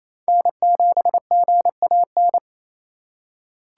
Morse code, heard as a keyed tone, about 28 wpm, 700 Hz. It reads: N7GAN
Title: USA Call Signs - 5 Characters - 2x Character Spacing - 28wpm